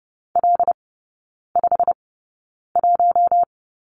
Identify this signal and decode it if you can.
Morse code, heard as a keyed tone, about 30 wpm, spaced as for 9 wpm, 700 Hz. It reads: L51
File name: course / 30:10wpm / 087 Lesson 15e - Number 5 - Three Characters 30wpm